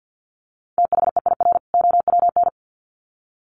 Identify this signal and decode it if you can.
Morse code, heard as a keyed tone, about 50 wpm, 700 Hz. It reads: THEIR OWN
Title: Sets of 2 Words 50wpm